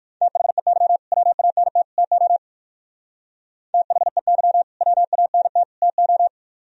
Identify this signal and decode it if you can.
Morse code, heard as a keyed tone, about 58 wpm, 700 Hz. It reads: THEY WANT TO THEY WANT TO